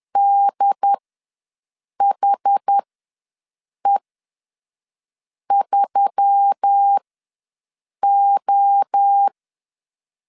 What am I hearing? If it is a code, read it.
Morse code, heard as a keyed tone, about 10 wpm, spaced as for 5 wpm, 800 Hz. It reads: DHE3O
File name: LEZIONI / 5 LBVC / 5 LBVC ws 10wpm 3733